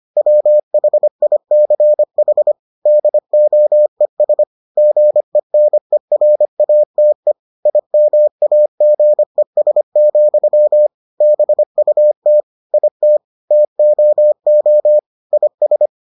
Morse code, heard as a keyed tone, about 25 wpm, 600 Hz. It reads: WHICH DOES GENERATE IMAGES, BUT IT TOO IS